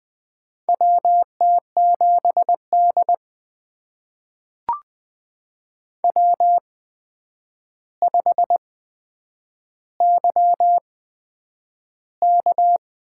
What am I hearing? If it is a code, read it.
Morse code, heard as a keyed tone, about 20 wpm, spaced as for 9 wpm, 700 Hz. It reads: WT7D W5YK
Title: USA Call Signs - 4 Characters - 8x Character Spacing - 20wpm